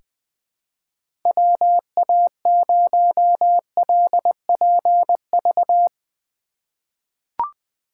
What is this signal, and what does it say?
Morse code, heard as a keyed tone, about 20 wpm, 700 Hz. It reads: WA0LPV
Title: K1USN SST QSO Practice - 20wpm